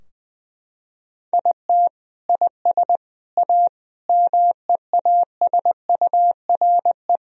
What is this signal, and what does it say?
Morse code, heard as a keyed tone, about 20 wpm, 700 Hz. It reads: IT IS A MEASURE
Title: Sentences from Top 1000 Words - Review Part 2 - 20wpm